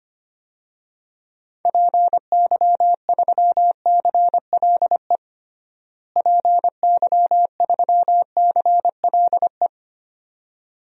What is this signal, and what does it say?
Morse code, heard as a keyed tone, about 25 wpm, 700 Hz. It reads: PY3CLE PY3CLE
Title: DXCC-100 Call Signs - 6 Characters - Repeated 2x - 25wpm